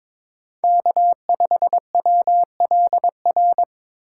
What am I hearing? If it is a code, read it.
Morse code, heard as a keyed tone, about 22 wpm, 700 Hz. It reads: K5WLR
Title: USA Call Signs - 5 Characters - 3x Character Spacing - 22wpm